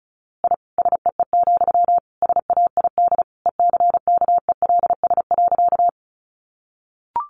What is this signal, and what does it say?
Morse code, heard as a keyed tone, about 34 wpm, 700 Hz. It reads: I SEE, SAID ECKELS.
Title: A Sound Of Thunder 35wpm